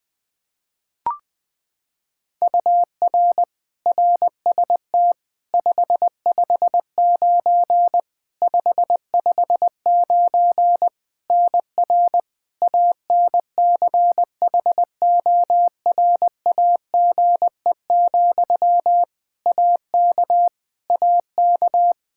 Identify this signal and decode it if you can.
Morse code, heard as a keyed tone, about 21 wpm, 700 Hz. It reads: UR RST 559 559 NR ANCHORAGE, AK AK
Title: Practice QSO - Exchange 1 - Beginning - 20wpm